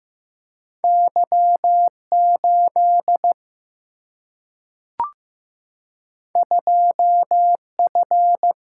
Morse code, heard as a keyed tone, about 15 wpm, 700 Hz. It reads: Y 8 2 F
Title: Single-Letter Single-Number 15wpm